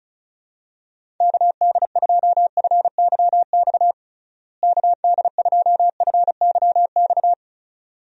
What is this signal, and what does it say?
Morse code, heard as a keyed tone, about 36 wpm, 700 Hz. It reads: KD2FYX KD2FYX
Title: USA Call Signs - 6 Characters - Repeated 2x - 35wpm